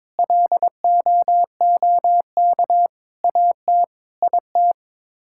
Morse code, heard as a keyed tone, about 22 wpm, 700 Hz. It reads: LOOK AT IT
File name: practice / 22wpm / Sets of 7 Words - 22wpm